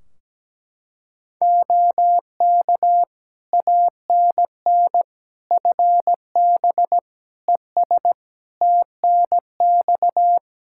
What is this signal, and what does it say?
Morse code, heard as a keyed tone, about 17 wpm, 700 Hz. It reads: OK ANN FB ES TNX
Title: K1USN SST QSO Practice 17wpm